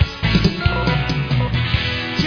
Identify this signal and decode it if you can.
There is no code.